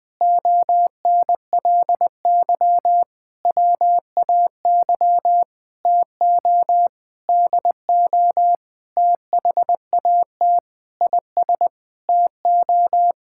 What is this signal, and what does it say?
Morse code, heard as a keyed tone, about 20 wpm, 700 Hz. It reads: ONLY WAY TO DO THAT IS TO
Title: Sentences with 9-Word Phrases - 20wpm